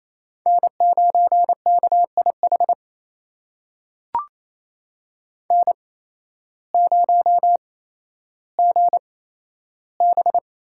Morse code, heard as a keyed tone, about 27 wpm, spaced as for 13 wpm, 700 Hz. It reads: N9KIH N0GB
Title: USA Call Signs - 5 Characters - 8x Character Spacing - 28wpm